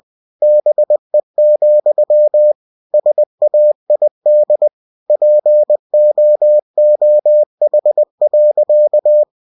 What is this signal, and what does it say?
Morse code, heard as a keyed tone, about 20 wpm, 600 Hz. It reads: BE, SAID POOH.
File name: winnie-20wpm